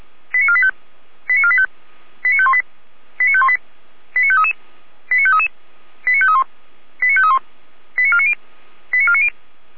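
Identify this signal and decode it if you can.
Five-tone selective call (ZVEI), heard as quick runs of five tones, tones of about 70 ms. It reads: 89485 89485 89518 89518 89533 89533 89531 89531 89490 89490
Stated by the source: ZVEI five-tone selective call received via SDR.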